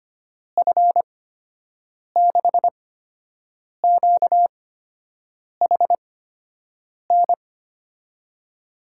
Morse code, heard as a keyed tone, about 25 wpm, spaced as for 7 wpm, 700 Hz. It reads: F6QHN
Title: USA Call Signs - 8x Character Spacing - 25wpm